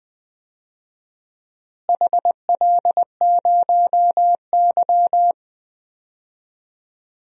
Morse code, heard as a keyed tone, about 20 wpm, 700 Hz. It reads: HL0Y
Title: DXCC-100 Call Signs - 4 Characters - Repeated 2x - 20wpm